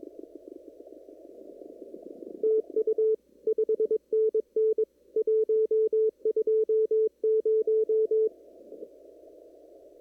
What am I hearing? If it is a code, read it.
Morse code, heard as a keyed tone, about 22 wpm, 440 Hz. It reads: TU 5NN 120
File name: CW / kt5j wpx cw-op-k5pi